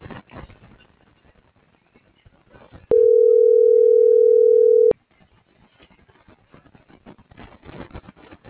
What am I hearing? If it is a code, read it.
Ringback tone (North America), 440 + 480 Hz.
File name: CENTRAL TERRESTRE / out vane.lazaro - .33621